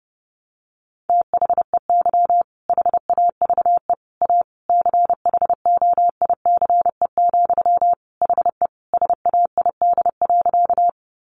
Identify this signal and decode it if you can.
Morse code, heard as a keyed tone, about 30 wpm, 700 Hz. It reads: THEY HAVE A CHOICE, HE SAID.
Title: Sentences from Top 900 Words 30wpm